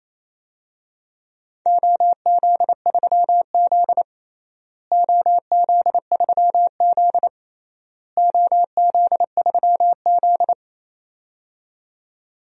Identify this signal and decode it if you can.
Morse code, heard as a keyed tone, about 27 wpm, 700 Hz. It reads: OZ3Z OZ3Z OZ3Z
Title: DXCC-100 Call Signs - 4 Characters - Repeated 3x - 28wpm